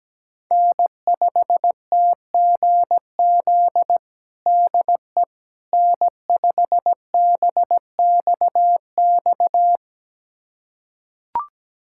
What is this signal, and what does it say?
Morse code, heard as a keyed tone, about 17 wpm, 700 Hz. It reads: N5TGZ DE N5BXX
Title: USA Callsign DE Callsign - 5 Characters - 17wpm